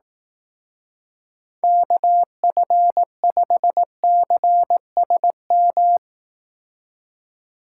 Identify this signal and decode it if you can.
Morse code, heard as a keyed tone, about 18 wpm, 700 Hz. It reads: KF5CSM
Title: Speed Racing - USA 2x3 Call Signs 15wpm